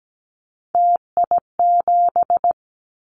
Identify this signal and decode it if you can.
Morse code, heard as a keyed tone, about 17 wpm, 700 Hz. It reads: TI7
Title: Callsign Prefixes for Top 100 DXCC - 17wpm